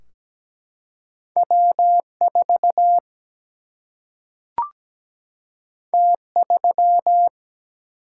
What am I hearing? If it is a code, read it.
Morse code, heard as a keyed tone, about 17 wpm, 700 Hz. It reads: W4 T3
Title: Single-Letter Single-Number 17wpm